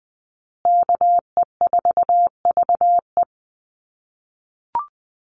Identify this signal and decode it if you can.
Morse code, heard as a keyed tone, about 20 wpm, 700 Hz. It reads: KE4VE
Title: USA Call Signs Repeated 3x 20wpm